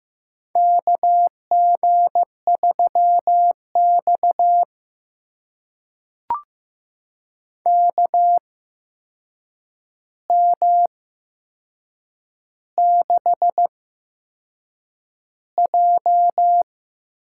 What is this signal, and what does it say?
Morse code, heard as a keyed tone, about 15 wpm, spaced as for 6 wpm, 700 Hz. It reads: KG3X KM6J
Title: USA Call Signs - 8x Character Spacing - 15wpm